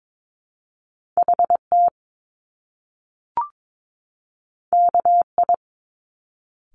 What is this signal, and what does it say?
Morse code, heard as a keyed tone, about 22 wpm, 700 Hz. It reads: HT KI
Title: Top Two Letter Combinations in Words 22wpm